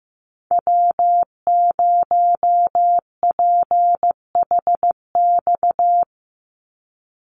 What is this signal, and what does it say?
Morse code, heard as a keyed tone, about 15 wpm, 700 Hz. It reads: W0PHX
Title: Call Signs Repeated 3x - 15wpm